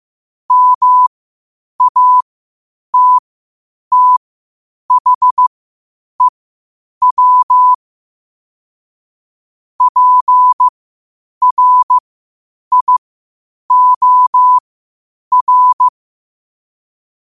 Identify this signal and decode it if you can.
Morse code, heard as a keyed tone, about 15 wpm, spaced as for 8 wpm, 1000 Hz. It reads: MATTHEW PRIOR
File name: qotdfiles / 8wpm